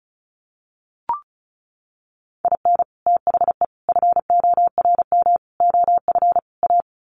Morse code, heard as a keyed tone, about 34 wpm, 700 Hz. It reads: IN THE FORM OF A
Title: Sets of 5 Words 35wpm